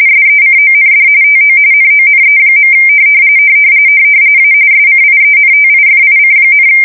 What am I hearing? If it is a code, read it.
RTTY, about 45.45 baud. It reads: SECOND TO NONE. A SCIENTIFIC,
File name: fsk170 / USS-New-Jersey